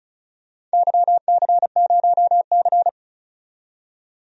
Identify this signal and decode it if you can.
Morse code, heard as a keyed tone, about 36 wpm, 700 Hz. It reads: YC0C
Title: DXCC-100 Call Signs - 4 Characters - 35wpm